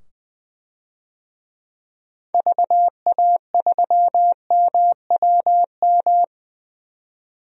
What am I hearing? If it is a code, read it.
Morse code, heard as a keyed tone, about 20 wpm, 700 Hz. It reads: VA3MWM
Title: DXCC-100 Call Signs - 6 Characters - 20wpm